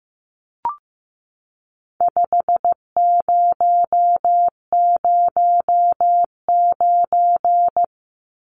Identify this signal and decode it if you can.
Morse code, heard as a keyed tone, about 15 wpm, 700 Hz. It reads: 5009
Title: Numbers - 4-Digit 15wpm